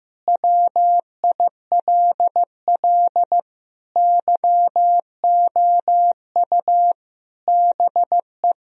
Morse code, heard as a keyed tone, about 15 wpm, 700 Hz. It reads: WILL YOU BE
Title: Sentences from Top 1000 Words - Review Part 2 - 15wpm